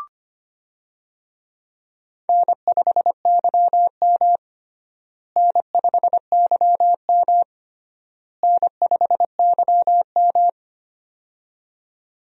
Morse code, heard as a keyed tone, about 25 wpm, 700 Hz. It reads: N5YM N5YM N5YM
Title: USA Call Signs - 4 Characters - Repeated 3x - 25wpm